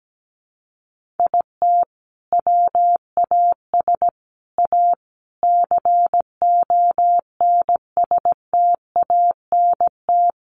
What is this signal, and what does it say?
Morse code, heard as a keyed tone, about 17 wpm, 700 Hz. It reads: IT WAS A CONSTANT
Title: Sentences from Top 1500 Words 17wpm